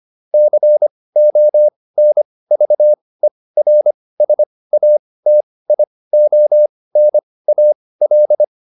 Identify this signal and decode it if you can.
Morse code, heard as a keyed tone, about 25 wpm, spaced as for 18 wpm, 600 Hz. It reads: CONVERSATIONAL